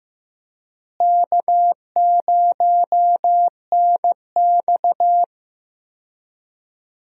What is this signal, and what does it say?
Morse code, heard as a keyed tone, about 16 wpm, 700 Hz. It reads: K0NX
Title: USA Call Signs - 4 Characters - Repeated 2x - 15wpm